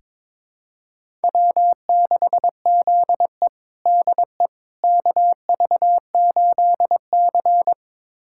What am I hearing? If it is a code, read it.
Morse code, heard as a keyed tone, about 22 wpm, 700 Hz. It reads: W6ZE DE KV8C
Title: USA Callsign DE Callsign - 4 Characters - 22wpm